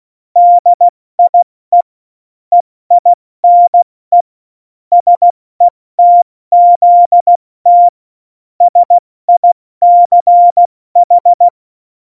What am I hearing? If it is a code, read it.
Morse code, heard as a keyed tone, about 16 wpm, spaced as for 14 wpm, 700 Hz. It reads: DIE EINE SETZT SICH
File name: Morsekurs / Witze / Witz 22. Mai 2018, 70 F80